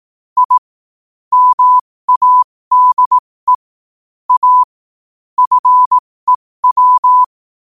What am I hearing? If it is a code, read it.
Morse code, heard as a keyed tone, about 18 wpm, spaced as for 15 wpm, 1000 Hz. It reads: I MADE A FEW
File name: qotdfiles / 15wpm